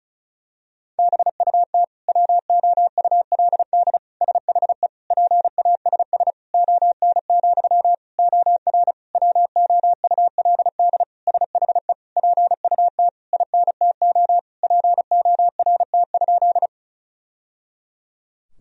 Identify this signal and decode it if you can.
Morse code, heard as a keyed tone, about 36 wpm, 700 Hz. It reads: BUT WOULD SHE PASS ON, OR WOULD SHE PUT INTO PORT?
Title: Sentences from Top 1000 Words - Review Part 1 - 35wpm